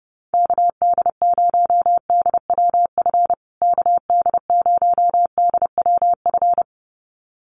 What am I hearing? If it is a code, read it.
Morse code, heard as a keyed tone, about 30 wpm, 700 Hz. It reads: KD0DWF KD0DWF
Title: USA Call Signs Repeated 2x 30wpm